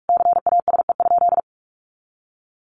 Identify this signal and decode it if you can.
Morse code, heard as a keyed tone, about 44 wpm, 700 Hz. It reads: CASE?
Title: Sentences from Top 1200 Words 45wpm